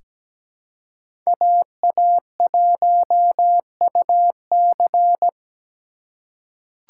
Morse code, heard as a keyed tone, about 17 wpm, 700 Hz. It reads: AA1UC
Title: K1USN SST QSO Practice - 17wpm